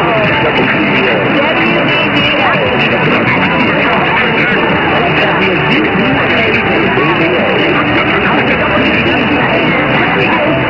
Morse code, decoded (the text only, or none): none